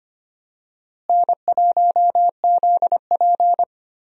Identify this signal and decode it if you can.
Morse code, heard as a keyed tone, about 25 wpm, 700 Hz. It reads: N1ZP
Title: USA Call Signs - 4 Characters - Repeated 3x - 25wpm